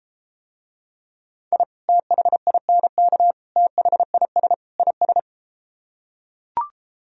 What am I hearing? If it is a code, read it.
Morse code, heard as a keyed tone, about 33 wpm, 700 Hz. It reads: I THINK THIS IS